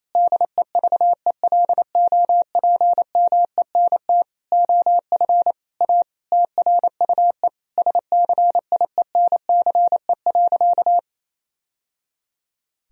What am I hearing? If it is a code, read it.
Morse code, heard as a keyed tone, about 28 wpm, 700 Hz. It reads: DEVELOPMENT OF A TRUE SCIENCE.
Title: Sentences from Top 1000 Words - Review Part 4 - 28wpm